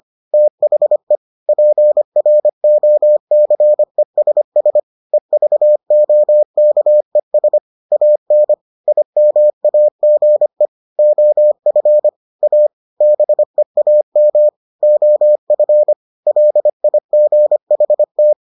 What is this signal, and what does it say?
Morse code, heard as a keyed tone, about 25 wpm, 600 Hz. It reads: THE PROCESS EVOKES AN IMAGE OF A BEAM OF LIGHT